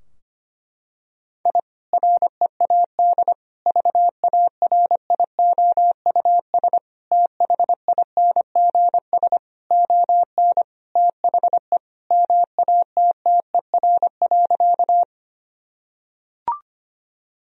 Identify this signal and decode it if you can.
Morse code, heard as a keyed tone, about 25 wpm, 700 Hz. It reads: I READ VARIOUS THINGS ON THE MATTER.
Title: Sentences from Top 1000 Words - Review Part 4 - 25wpm